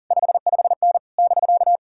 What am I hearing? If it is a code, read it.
Morse code, heard as a keyed tone, about 50 wpm, 700 Hz. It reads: 55N <BK>
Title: Speed-Racing Common QSO Phrases - 40wpm